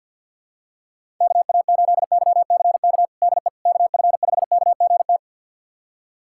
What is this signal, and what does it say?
Morse code, heard as a keyed tone, about 51 wpm, 700 Hz. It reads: KA8YCK DE KF5KGT
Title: USA Callsign DE Callsign - 6 Characters - 50wpm